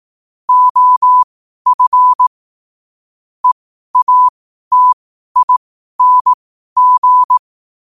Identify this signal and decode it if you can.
Morse code, heard as a keyed tone, about 18 wpm, spaced as for 12 wpm, 1000 Hz. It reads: OF EATING
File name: qotdfiles / 12wpm